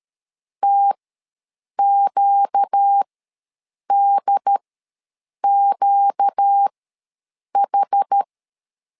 Morse code, heard as a keyed tone, about 13 wpm, spaced as for 7 wpm, 800 Hz. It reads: TQDQH
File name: LEZIONI / 4 Q ws 12wpm 1912